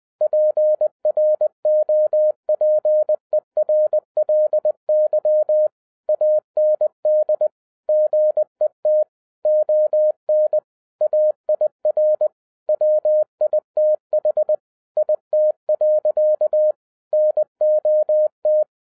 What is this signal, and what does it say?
Morse code, heard as a keyed tone, about 20 wpm, 610 Hz. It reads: PROPERLY AND GET ON AIR WITH IT. NOT
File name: 2011-07-mp3 / n08 A LOT HAS CHANGED 20wpm